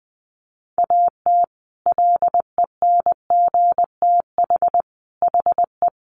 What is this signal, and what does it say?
Morse code, heard as a keyed tone, about 20 wpm, 700 Hz. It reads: AT LENGTH HE
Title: Sentences from Top 1000 Words 20wpm